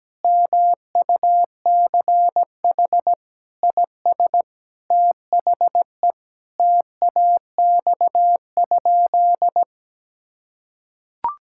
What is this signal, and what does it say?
Morse code, heard as a keyed tone, about 17 wpm, 700 Hz. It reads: MUCH IS THE TAX?
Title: Sentences from Top 1000 Words - Review Part 5 - 17wpm